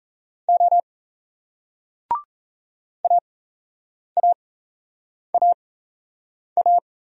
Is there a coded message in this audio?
Yes: Morse code from 0.5 to 6.8 s, heard as a keyed tone, about 38 wpm, spaced as for 8 wpm, 700 Hz. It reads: O AAAA